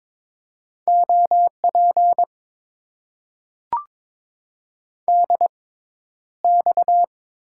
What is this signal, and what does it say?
Morse code, heard as a keyed tone, about 22 wpm, spaced as for 11 wpm, 700 Hz. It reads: OP DX
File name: Top 90 QSO Elements - 6x Character Spacing - 22wpm